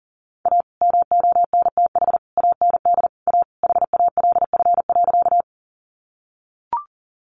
Morse code, heard as a keyed tone, about 38 wpm, 700 Hz. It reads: A MONTH AND A HALF.
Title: Sentences from Top 800 Words 40wpm